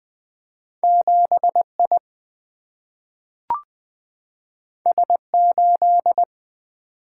Morse code, heard as a keyed tone, about 20 wpm, 700 Hz. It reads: 7I S8
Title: Single-Letter Single-Number 20wpm